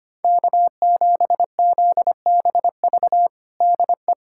Morse code, heard as a keyed tone, about 25 wpm, 700 Hz. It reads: K7ZBV DE
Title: USA Callsign DE Callsign - 5 Characters - 25wpm